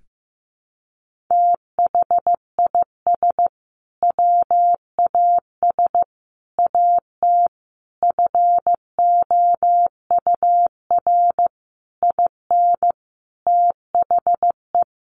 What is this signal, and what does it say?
Morse code, heard as a keyed tone, about 15 wpm, 700 Hz. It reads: THIS WAS AT FOUR IN THE
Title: Sentences from Top 1500 Words 15wpm